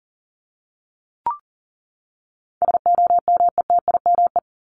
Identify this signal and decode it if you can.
Morse code, heard as a keyed tone, about 40 wpm, 700 Hz. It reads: SOMETIME
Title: ES:S Suffix - Top 100 Words with Base Words 40wpm